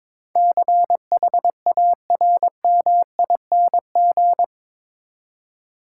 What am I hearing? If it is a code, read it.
Morse code, heard as a keyed tone, about 22 wpm, 700 Hz. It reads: CHARMING
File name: practice / 22wpm / Eight Letter Words - 22wpm